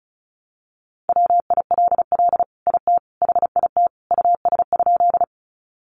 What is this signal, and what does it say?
Morse code, heard as a keyed tone, about 34 wpm, 700 Hz. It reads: WILL IT HIT US?
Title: Sentences from Top 1200 Words 35wpm